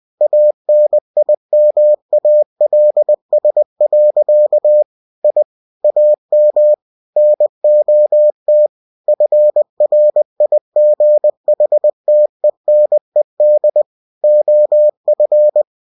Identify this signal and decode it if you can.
Morse code, heard as a keyed tone, about 20 wpm, 600 Hz. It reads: ANIMALS. I AM NOT FRIGHTENED OF
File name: cw-books / winnie-the-pooh / winnie-20wpm-page-0062